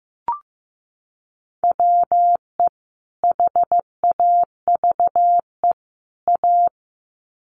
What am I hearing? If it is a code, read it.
Morse code, heard as a keyed tone, about 15 wpm, 700 Hz. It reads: WE HAVE A
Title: Sets of 3 Words 15wpm